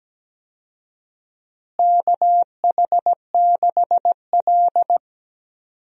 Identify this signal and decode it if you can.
Morse code, heard as a keyed tone, about 17 wpm, 700 Hz. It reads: KH6L
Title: DXCC-100 Call Signs - 4 Characters - Repeated 3x - 17wpm